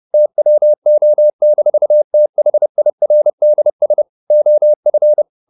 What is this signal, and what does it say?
Morse code, heard as a keyed tone, about 30 wpm, 600 Hz. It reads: TWO-THIRDS OF